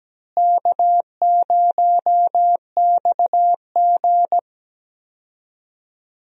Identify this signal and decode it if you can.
Morse code, heard as a keyed tone, about 17 wpm, 700 Hz. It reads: K0XG
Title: USA Call Signs - 4 Characters - Repeated 3x - 17wpm